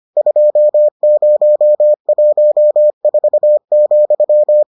Morse code, heard as a keyed tone, about 25 wpm, 600 Hz. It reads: 2014,